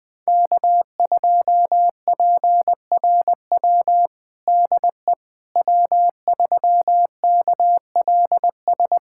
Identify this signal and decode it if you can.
Morse code, heard as a keyed tone, about 20 wpm, 700 Hz. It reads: K2PRW DE W3KLS
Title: USA Callsign DE Callsign - 5 Characters - 20wpm